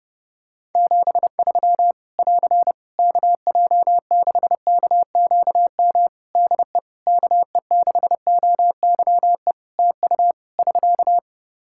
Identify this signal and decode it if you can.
Morse code, heard as a keyed tone, about 29 wpm, 700 Hz. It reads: 73 <AR> KJ6KQM DE KE6OYE TU <SK>
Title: Practice QSO - Exchange 4 - Ending - 30wpm